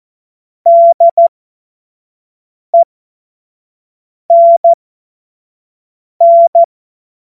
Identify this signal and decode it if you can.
Morse code, heard as a keyed tone, about 14 wpm, spaced as for 5 wpm, 680 Hz. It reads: DENN
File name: Morsekurs / Witze / Witz 06. September 2022 25 F70